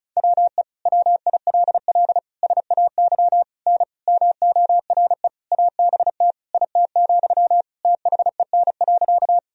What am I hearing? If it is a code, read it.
Morse code, heard as a keyed tone, about 34 wpm, 700 Hz. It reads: WE WILL SAY N MORE ABT IT, THEN.
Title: Sentences from Top 200 Words and Abbreviations 35wpm